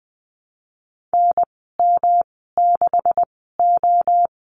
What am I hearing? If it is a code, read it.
Morse code, heard as a keyed tone, about 20 wpm, spaced as for 15 wpm, 700 Hz. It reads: NM6O
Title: 140 Lesson 23g - Letter f - Call Signs 20wpm